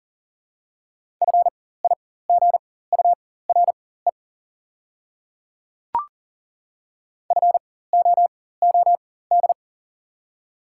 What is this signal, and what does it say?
Morse code, heard as a keyed tone, about 40 wpm, spaced as for 19 wpm, 700 Hz. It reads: FIGURE FOOD
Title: 4x Character Spacing - Top 500 Words - 40wpm